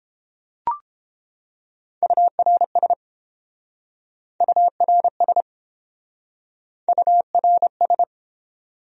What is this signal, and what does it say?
Morse code, heard as a keyed tone, about 32 wpm, 700 Hz. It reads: URS URS URS